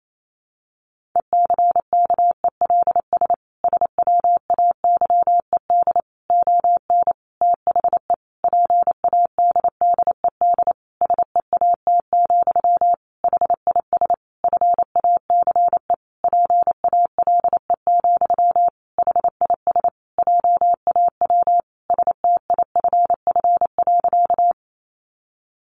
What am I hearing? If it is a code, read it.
Morse code, heard as a keyed tone, about 28 wpm, 700 Hz. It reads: ECKELS SWAYED ON THE PADDED SEAT, HIS FACE PALE, HIS JAW STIFF.
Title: A Sound Of Thunder 28wpm